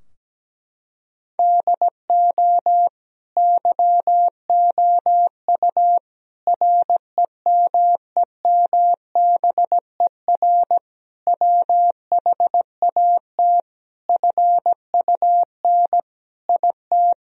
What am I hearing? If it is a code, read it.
Morse code, heard as a keyed tone, about 17 wpm, 700 Hz. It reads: DO YOU REMEMBER WHAT FUN IT
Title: Sentences from Top 1000 Words - Review Part 4 - 17wpm